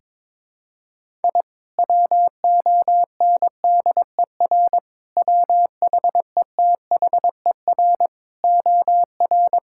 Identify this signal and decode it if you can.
Morse code, heard as a keyed tone, about 23 wpm, 700 Hz. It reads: I WONDER WHETHER OR
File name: practice / 22wpm / Sentences with 3-Word Phrases - 22wpm